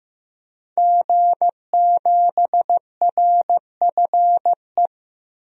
Morse code, heard as a keyed tone, about 16 wpm, 700 Hz. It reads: G7RFE
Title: DXCC-100 Call Signs - 5 Characters - 15wpm